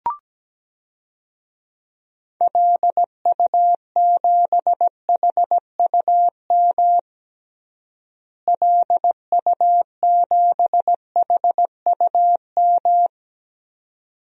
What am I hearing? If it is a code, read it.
Morse code, heard as a keyed tone, about 17 wpm, 700 Hz. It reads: LU7HUM LU7HUM
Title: DXCC-100 Call Signs - 6 Characters - Repeated 2x - 17wpm